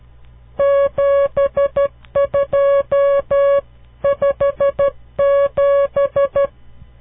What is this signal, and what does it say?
Morse code, heard as a keyed tone, about 12 wpm, 560 Hz. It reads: 7257
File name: CW Learning Programs / farnsworth / unit 5